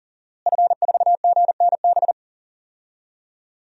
Morse code, heard as a keyed tone, about 40 wpm, 700 Hz. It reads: F4GNB
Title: DXCC-100 Call Signs - 5 Characters - Repeated 2x - 40wpm